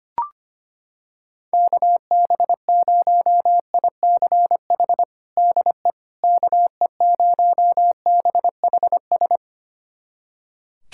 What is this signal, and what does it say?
Morse code, heard as a keyed tone, about 25 wpm, 700 Hz. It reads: KB0ICH DE KE0BHS
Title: USA Callsign DE Callsign - 6 Characters - 25wpm